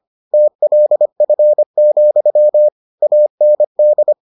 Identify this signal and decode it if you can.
Morse code, heard as a keyed tone, about 25 wpm, 600 Hz. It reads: TLF, AND